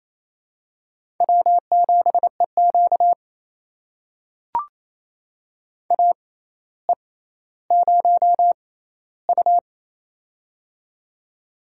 Morse code, heard as a keyed tone, about 28 wpm, spaced as for 14 wpm, 700 Hz. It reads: W7EQ AE0U
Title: USA Call Signs - 4 Characters - 6x Character Spacing - 28wpm